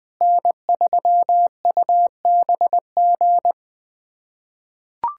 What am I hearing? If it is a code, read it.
Morse code, heard as a keyed tone, about 20 wpm, 700 Hz. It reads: N3UBG
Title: USA Callsign DE Callsign - 5 Characters - 20wpm